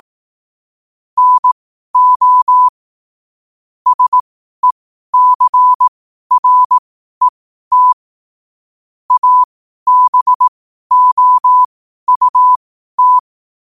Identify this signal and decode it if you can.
Morse code, heard as a keyed tone, about 18 wpm, spaced as for 12 wpm, 1000 Hz. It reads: NO SECRET ABOUT
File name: qotdfiles / 12wpm